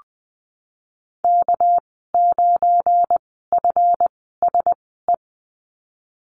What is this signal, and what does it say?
Morse code, heard as a keyed tone, about 20 wpm, spaced as for 15 wpm, 700 Hz. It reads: K9FSE
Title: 219 Lesson 35g - Number 0 - Call Signs 20wpm